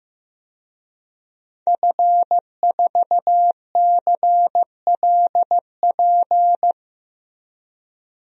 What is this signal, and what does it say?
Morse code, heard as a keyed tone, about 15 wpm, 700 Hz. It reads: F4CLP
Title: DXCC-100 Call Signs - 5 Characters - 15wpm